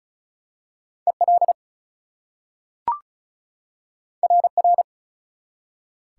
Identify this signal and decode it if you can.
Morse code, heard as a keyed tone, about 36 wpm, 700 Hz. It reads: EL RR
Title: Top 180 QSO Elements 35wpm